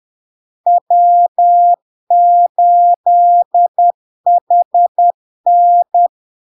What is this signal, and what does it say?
Morse code, heard as a keyed tone, about 10 wpm, 700 Hz. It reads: W8HN
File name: LARK-FD-2022-Log-in-CW-10wpm-700-Hz